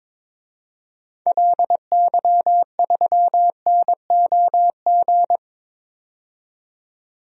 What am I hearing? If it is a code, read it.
Morse code, heard as a keyed tone, about 22 wpm, 700 Hz. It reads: LY3NOG
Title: DXCC-100 Call Signs - 6 Characters - 22wpm